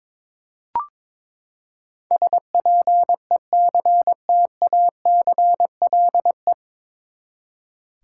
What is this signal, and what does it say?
Morse code, heard as a keyed tone, about 22 wpm, 700 Hz. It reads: SPECTACLE
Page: Morse Code Ninja practice 22wpm